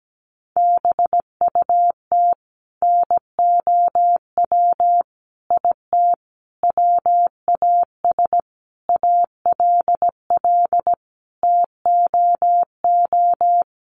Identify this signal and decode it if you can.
Morse code, heard as a keyed tone, about 17 wpm, 700 Hz. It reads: BUT NOW IT WAS ALL TOO
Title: Sentences from Top 700 Words 17wpm